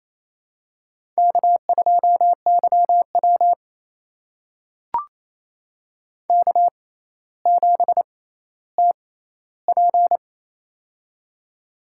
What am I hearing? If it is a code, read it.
Morse code, heard as a keyed tone, about 27 wpm, spaced as for 14 wpm, 700 Hz. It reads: K2YW K7TP
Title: USA Call Signs - 4 Characters - 6x Character Spacing - 28wpm